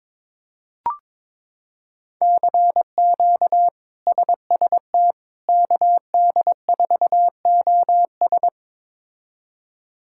Morse code, heard as a keyed tone, about 22 wpm, 700 Hz. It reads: CQ SST KD4OS
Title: K1USN SST QSO Practice - 22wpm